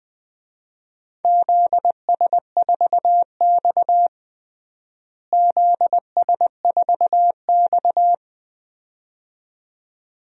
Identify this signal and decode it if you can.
Morse code, heard as a keyed tone, about 20 wpm, 700 Hz. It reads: ZS4X ZS4X
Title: DXCC-100 Call Signs - 4 Characters - Repeated 2x - 20wpm